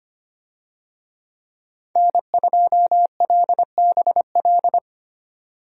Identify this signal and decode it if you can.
Morse code, heard as a keyed tone, about 25 wpm, 700 Hz. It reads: N2LBL
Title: USA Call Signs - 5 Characters - 25wpm